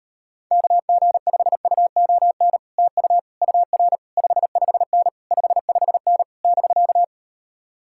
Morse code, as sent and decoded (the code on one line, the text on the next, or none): -.- --. ..... ..- --- -. / - ..- / ..- .-. / ..... ..... -. / ..... ..... -. / -...-.-
KG5UON TU UR 55N 55N <BK>